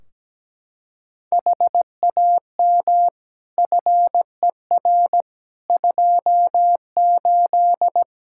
Morse code, heard as a keyed tone, about 17 wpm, 700 Hz. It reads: HAM FER 28
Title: Practice QSO - Exchange 3 - More Info - 17wpm